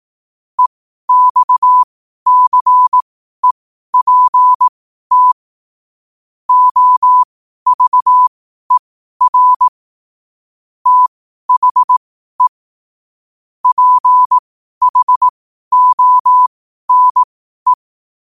Morse code, heard as a keyed tone, about 18 wpm, spaced as for 12 wpm, 1000 Hz. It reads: EXCEPT OVER THE PHONE